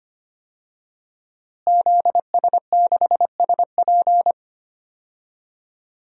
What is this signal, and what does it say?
Morse code, heard as a keyed tone, about 25 wpm, 700 Hz. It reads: ZS6SP
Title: DXCC-100 Call Signs - 5 Characters - Repeated 3x - 25wpm